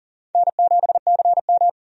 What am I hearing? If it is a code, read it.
Morse code, heard as a keyed tone, about 40 wpm, 700 Hz. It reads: N7CM
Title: USA Call Signs - 4 Characters - 40wpm